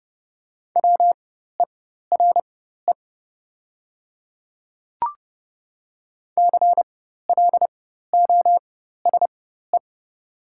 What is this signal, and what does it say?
Morse code, heard as a keyed tone, about 30 wpm, spaced as for 14 wpm, 690 Hz. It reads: WERE CLOSE
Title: 4x Character Spacing - Top 300 Words - 30wpm